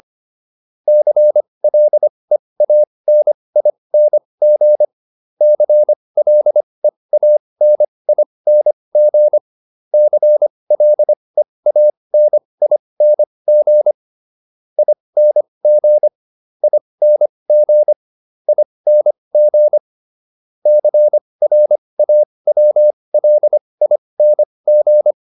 Morse code, heard as a keyed tone, about 25 wpm, spaced as for 20 wpm, 600 Hz. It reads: CLEANING CLEANING CLEANING ING ING ING CRAWLING